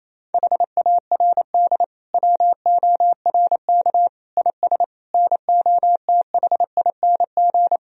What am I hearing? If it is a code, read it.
Morse code, heard as a keyed tone, about 28 wpm, 700 Hz. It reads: HARD WORK IS NOTHING
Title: Sentences from Top 1000 Words - Review Part 1 - 28wpm